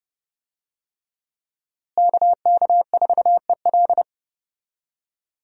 Morse code, heard as a keyed tone, about 30 wpm, 700 Hz. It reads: KK4EL
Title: USA Call Signs - 5 Characters - Repeated 3x - 30wpm